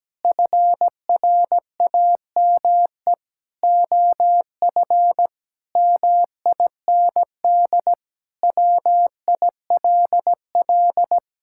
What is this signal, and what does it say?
Morse code, heard as a keyed tone, about 17 wpm, 700 Hz. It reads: FRAME OF MIND WILL
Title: Sentences from Top 1000 Words - Review Part 6 - 17wpm